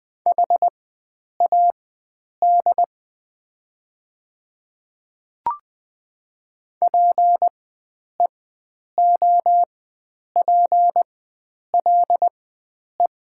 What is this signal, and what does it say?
Morse code, heard as a keyed tone, about 20 wpm, spaced as for 9 wpm, 690 Hz. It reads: HAD PEOPLE
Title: 4x Character Spacing - Top 100 Words - 20wpm